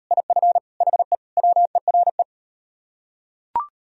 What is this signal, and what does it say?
Morse code, heard as a keyed tone, about 37 wpm, 700 Hz. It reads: IF HE WERE